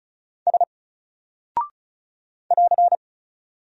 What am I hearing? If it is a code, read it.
Morse code, heard as a keyed tone, about 36 wpm, 700 Hz. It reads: S <AR>
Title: Single Character (Letter:Number:Punctuation:Prosign) Alt 35wpm